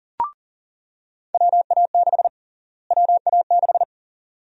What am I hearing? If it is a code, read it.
Morse code, heard as a keyed tone, about 40 wpm, 700 Hz. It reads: WA6 WA6
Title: USA Call Sign Prefixes - Repeated 2x - 40wpm